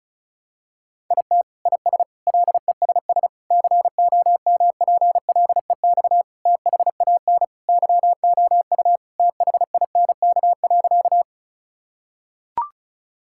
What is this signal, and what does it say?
Morse code, heard as a keyed tone, about 34 wpm, 700 Hz. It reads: IT IS LESS COMPLEX THAN YOU THINK.
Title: Sentences from Top 1000 Words - Review Part 6 - 35wpm